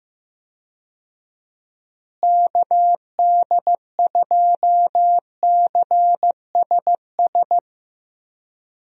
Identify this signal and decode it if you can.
Morse code, heard as a keyed tone, about 15 wpm, 700 Hz. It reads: KD2CSS
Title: USA Call Signs - 6 Characters - Repeated 2x - 15wpm